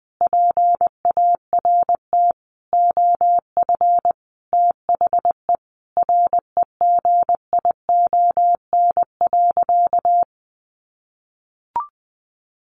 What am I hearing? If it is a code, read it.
Morse code, heard as a keyed tone, about 20 wpm, 700 Hz. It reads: PART OF THE REGION.
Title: Sentences from Top 1500 Words 20wpm